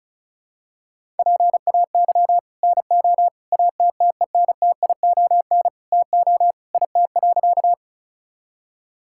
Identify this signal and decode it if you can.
Morse code, heard as a keyed tone, about 36 wpm, 700 Hz. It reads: PAY NO ATTENTION TO IT.
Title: Sentences from Top 1000 Words - Review Part 6 - 35wpm